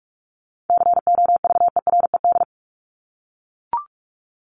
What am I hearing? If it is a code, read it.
Morse code, heard as a keyed tone, about 44 wpm, 700 Hz. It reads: COVERED
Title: ED Suffix - Top 100 Words with Base Words 45wpm